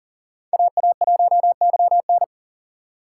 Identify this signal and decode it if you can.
Morse code, heard as a keyed tone, about 40 wpm, 700 Hz. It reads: AA1YN
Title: USA Call Signs - 6x Character Spacing - 40wpm